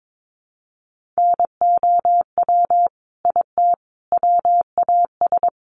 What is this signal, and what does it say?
Morse code, heard as a keyed tone, about 22 wpm, 700 Hz. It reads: NOW IT WAS
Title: Sentences from Top 200 Words 22wpm